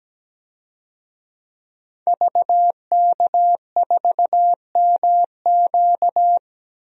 Morse code, heard as a keyed tone, about 17 wpm, 700 Hz. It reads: VK4MQ
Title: DXCC-100 Call Signs - 5 Characters - 17wpm